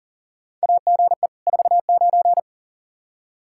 Morse code, heard as a keyed tone, about 38 wpm, 700 Hz. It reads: AGE 49
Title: Practice QSO - Exchange 3 - More Info - 40wpm